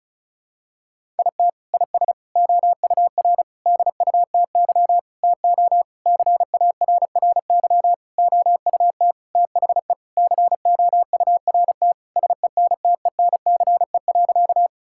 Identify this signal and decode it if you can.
Morse code, heard as a keyed tone, about 36 wpm, 700 Hz. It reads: IT IS OUR DUTY TO CARRY OUT THE COURT SENTENCE.
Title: Sentences from Top 1600 Words 35wpm